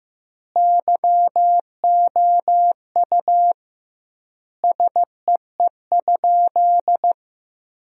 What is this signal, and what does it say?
Morse code, heard as a keyed tone, about 15 wpm, 700 Hz. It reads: YOU SEE?
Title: 2x Word-Spacing - Sentences from Top 100 Words - 15wpm